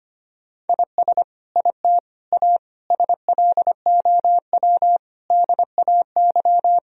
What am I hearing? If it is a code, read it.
Morse code, heard as a keyed tone, about 25 wpm, 700 Hz. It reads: IS IT A SLOW DAY
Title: Sentences from Top 1000 Words - Review Part 2 - 25wpm